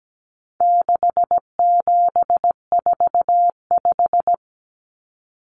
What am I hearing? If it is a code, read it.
Morse code, heard as a keyed tone, about 17 wpm, 700 Hz. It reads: 6745
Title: Numbers - 4-Digit 17wpm